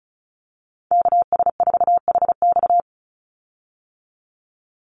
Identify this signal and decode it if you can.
Morse code, heard as a keyed tone, about 34 wpm, 700 Hz. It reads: KS4HX
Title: USA Call Signs 35wpm